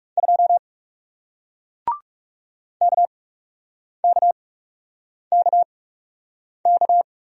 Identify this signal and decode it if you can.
Morse code, heard as a keyed tone, about 37 wpm, spaced as for 8 wpm, 700 Hz. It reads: 2 KKKK
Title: Speed-Racing Single Letter-Number 30wpm